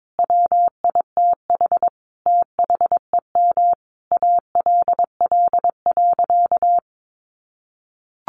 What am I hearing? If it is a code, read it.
Morse code, heard as a keyed tone, about 22 wpm, 700 Hz. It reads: WITH THEM ALL.
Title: Sentences from Top 1400 Words 22wpm